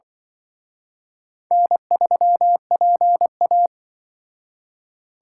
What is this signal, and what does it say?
Morse code, heard as a keyed tone, about 24 wpm, 700 Hz. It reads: N3PA
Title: Speed Racing - USA 1x2 Call Signs 20wpm